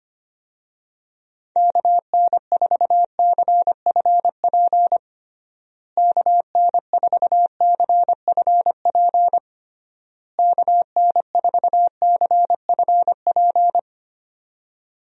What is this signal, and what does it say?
Morse code, heard as a keyed tone, about 25 wpm, 700 Hz. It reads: KN4CFP KN4CFP KN4CFP
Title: USA Call Signs - 6 Characters - Repeated 3x - 25wpm